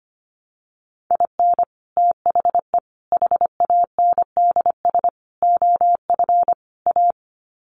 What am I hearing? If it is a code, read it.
Morse code, heard as a keyed tone, about 25 wpm, 700 Hz. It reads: IN THE HANDS OF A
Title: Sets of 5 Words Encore 25wpm